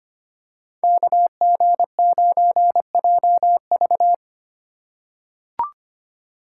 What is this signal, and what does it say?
Morse code, heard as a keyed tone, about 25 wpm, 700 Hz. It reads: KG9JV
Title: USA Call Signs - 2x Character Spacing - 25wpm